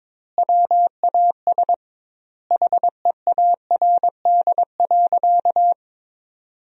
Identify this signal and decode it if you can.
Morse code, heard as a keyed tone, about 22 wpm, 700 Hz. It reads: WAS HEARD.
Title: 2x Word-Spacing - Sentences from Top 400 Words - 22wpm